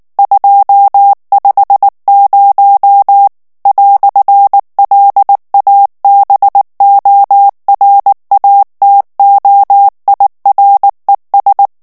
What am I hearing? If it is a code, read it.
Morse code, heard as a keyed tone, about 19 wpm, 800 Hz. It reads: 250 "LABORATOIRES
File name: morse mp3 / FRANCAIS TEXTE CLAIR 1 VITESSE 18 MOTS MINUTES